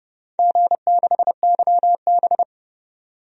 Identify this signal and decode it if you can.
Morse code, heard as a keyed tone, about 30 wpm, 700 Hz. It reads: G6YB
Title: DXCC-100 Call Signs - 4 Characters - 30wpm